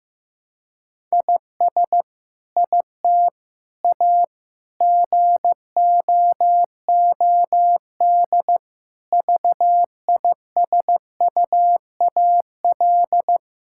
Morse code, heard as a keyed tone, about 16 wpm, 700 Hz. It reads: IS IT A GOOD VISUAL
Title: Sentences from Top 1000 Words - Review Part 6 - 15wpm